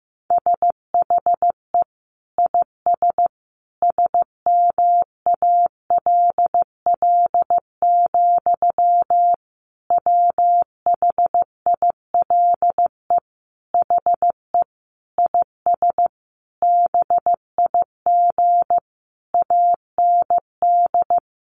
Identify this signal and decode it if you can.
Morse code, heard as a keyed tone, about 15 wpm, 700 Hz. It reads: SHE IS SMALL, WHILE HE IS BIG AND
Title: Sentences from Top 1300 Words 15wpm